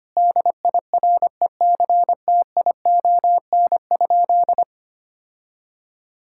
Morse code, heard as a keyed tone, about 25 wpm, 700 Hz. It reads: DIRECTION?
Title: Sentences from Top 1000 Words - Review Part 5 - 25wpm